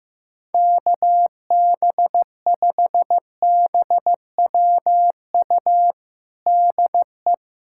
Morse code, heard as a keyed tone, about 15 wpm, 700 Hz. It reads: KB5BWU DE
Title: USA Callsign DE Callsign - 6 Characters - 15wpm